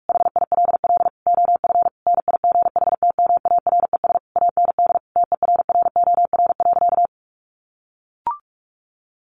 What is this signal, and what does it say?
Morse code, heard as a keyed tone, about 44 wpm, 700 Hz. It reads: HILL OF NIGHTMARES AND TERROR.
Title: A Sound Of Thunder 45wpm